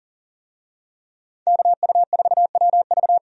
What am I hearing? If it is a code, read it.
Morse code, heard as a keyed tone, about 37 wpm, 700 Hz. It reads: KU4WV
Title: USA Call Signs - 5 Characters - 40wpm